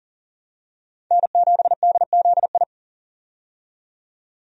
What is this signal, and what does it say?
Morse code, heard as a keyed tone, about 37 wpm, 700 Hz. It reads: N7DZI
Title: USA Call Signs - 5 Characters - 40wpm